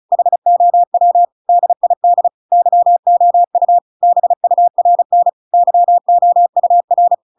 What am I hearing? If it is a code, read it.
Morse code, heard as a keyed tone, about 35 wpm, 700 Hz. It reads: HOW DID YOU BURN YOUR
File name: Pinocchio / Pinocchio ch7 35 35 ik1hge